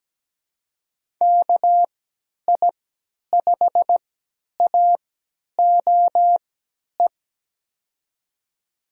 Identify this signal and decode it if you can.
Morse code, heard as a keyed tone, about 17 wpm, spaced as for 10 wpm, 700 Hz. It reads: KI5AOE
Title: USA Call Signs - 6 Characters - 3x Character Spacing - 17wpm